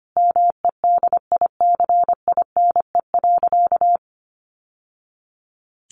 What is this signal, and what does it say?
Morse code, heard as a keyed tone, about 25 wpm, 700 Hz. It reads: MEDICINE.
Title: Sentences from Top 900 Words 25wpm